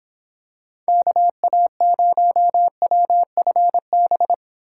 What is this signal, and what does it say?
Morse code, heard as a keyed tone, about 24 wpm, 700 Hz. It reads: KA0WFB